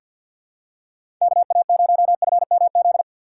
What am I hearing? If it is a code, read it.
Morse code, heard as a keyed tone, about 51 wpm, 700 Hz. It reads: KA0FMZ
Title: USA Call Signs - 6 Characters - 4x Character Spacing - 50wpm